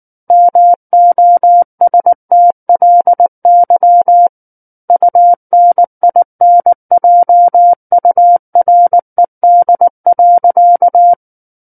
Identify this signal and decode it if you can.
Morse code, heard as a keyed tone, about 19 wpm, 700 Hz. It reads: MOSTLY UNINJURED.
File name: Book One / 10-Chapter 10 - 19 WPM